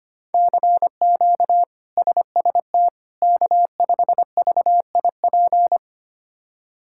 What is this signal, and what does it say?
Morse code, heard as a keyed tone, about 25 wpm, 700 Hz. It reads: CQ SST K5VIP
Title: K1USN SST QSO Practice - 25wpm